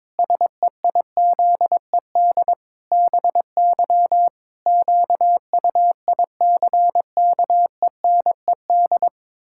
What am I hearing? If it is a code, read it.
Morse code, heard as a keyed tone, about 22 wpm, 700 Hz. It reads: SEIZED BY QUICKENED
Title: Pangram Challenge v2 - 22wpm